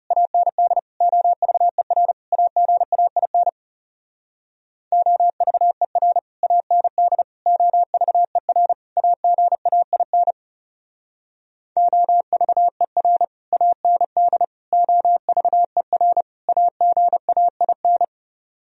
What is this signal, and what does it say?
Morse code, heard as a keyed tone, about 34 wpm, 700 Hz. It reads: AND OVER AGAIN OVER AND OVER AGAIN OVER AND OVER AGAIN
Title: Speed-Racing - Sets of 4 Words 30wpm